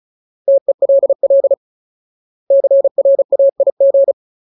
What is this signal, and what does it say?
Morse code, heard as a keyed tone, about 34 wpm, 550 Hz. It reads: TELL CRAIG
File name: Ultimate Word List / 35WPM Ultimate